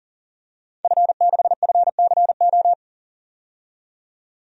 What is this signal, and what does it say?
Morse code, heard as a keyed tone, about 40 wpm, 700 Hz. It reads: F6FCO
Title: DXCC-100 Call Signs - 5 Characters - 40wpm